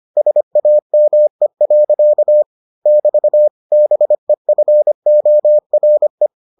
Morse code, heard as a keyed tone, about 25 wpm, 600 Hz. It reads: SAME. <BT> BEFORE